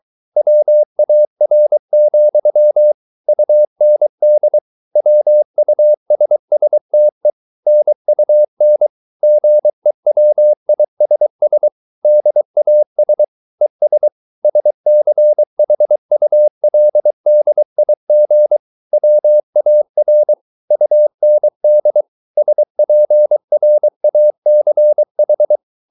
Morse code, heard as a keyed tone, about 23 wpm, 600 Hz. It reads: WAR, UND WUSSTE NUN GEWISS DAS ES SCHULDIG WAR UND SPRACH